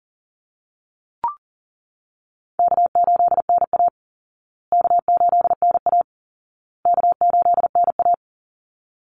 Morse code, heard as a keyed tone, about 40 wpm, 700 Hz. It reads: K8NA K8NA K8NA
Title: Easy Call Signs Repeated 3x - 40wpm